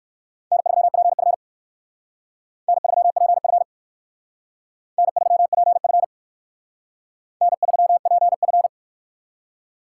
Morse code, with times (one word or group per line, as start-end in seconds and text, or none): 0.5-1.3 N3PF
2.7-3.6 N3PF
5.0-6.0 N3PF
7.4-8.7 N3PF